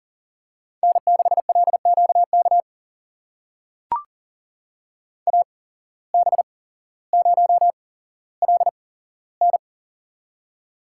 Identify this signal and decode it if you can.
Morse code, heard as a keyed tone, about 40 wpm, spaced as for 17 wpm, 700 Hz. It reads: N6LQK AB0LN
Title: USA Call Signs - 5 Characters - 8x Character Spacing - 40wpm